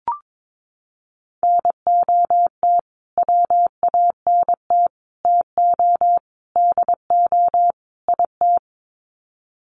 Morse code, heard as a keyed tone, about 22 wpm, 700 Hz. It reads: NOT WANT TO DO IT
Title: Sets of 5 Words Encore 22wpm